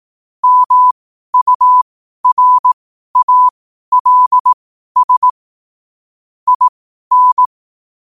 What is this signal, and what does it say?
Morse code, heard as a keyed tone, about 18 wpm, spaced as for 12 wpm, 1000 Hz. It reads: MURALS IN
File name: qotdfiles / 12wpm